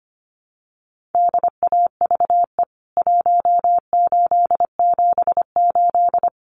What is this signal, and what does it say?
Morse code, heard as a keyed tone, about 25 wpm, 700 Hz. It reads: DAVE 1878
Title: CWT Exchange 25wpm